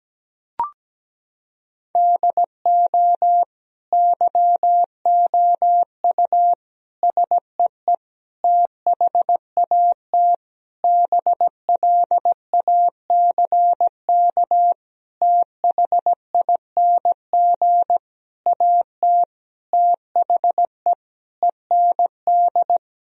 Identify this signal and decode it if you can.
Morse code, heard as a keyed tone, about 17 wpm, 700 Hz. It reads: DO YOU SEE THAT BLACK THING AT THE END
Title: Sentences from Top 1000 Words - Review Part 2 - 17wpm